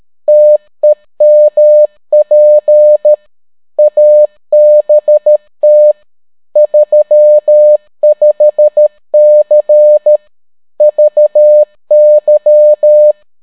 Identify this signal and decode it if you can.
Morse code, heard as a keyed tone, about 13 wpm, 600 Hz. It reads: TEMP ABT 35C VY